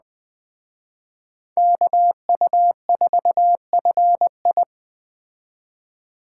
Morse code, heard as a keyed tone, about 20 wpm, 700 Hz. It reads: KU4FI
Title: Speed Racing - USA 2x2 Call Signs 17wpm